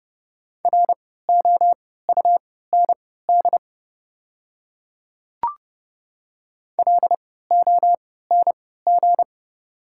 Morse code, heard as a keyed tone, about 30 wpm, spaced as for 17 wpm, 700 Hz. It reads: ROUND LONG
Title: 3x Character Spacing - Top 500 Words - 30wpm